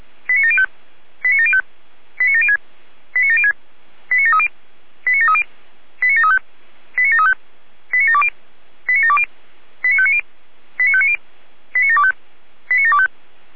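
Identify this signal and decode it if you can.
Five-tone selective call (ZVEI), heard as quick runs of five tones, tones of about 80 ms. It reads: 89705 89705 89706 89706 89730 89730 89735 89735 89720 89720 89590 89590 89725 89725
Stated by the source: ZVEI five-tone selective call 89750 received on an SDR.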